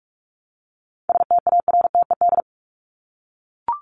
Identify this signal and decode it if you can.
Morse code, heard as a keyed tone, about 44 wpm, 700 Hz. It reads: STARTED
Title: ED Suffix - Top 100 Words with Base Words 45wpm